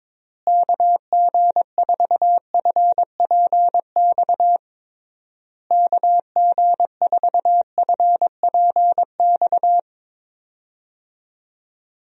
Morse code, heard as a keyed tone, about 22 wpm, 700 Hz. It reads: KG4FPX KG4FPX
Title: USA Call Signs - 6 Characters - Repeated 3x - 22wpm